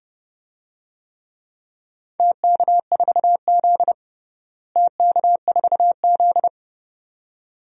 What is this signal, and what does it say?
Morse code, heard as a keyed tone, about 32 wpm, 700 Hz. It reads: TK4Z TK4Z
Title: DXCC-100 Call Signs - 4 Characters - Repeated 2x - 30wpm